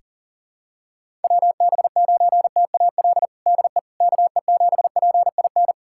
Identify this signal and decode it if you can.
Morse code, heard as a keyed tone, about 40 wpm, 700 Hz. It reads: WB9TAL DE KE7PIN
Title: USA Callsign DE Callsign - 6 Characters - 40wpm